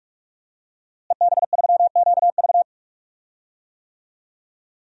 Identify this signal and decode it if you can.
Morse code, heard as a keyed tone, about 48 wpm, 700 Hz. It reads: EB3QV
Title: DXCC-100 Call Signs - 5 Characters - Repeated 3x - 45wpm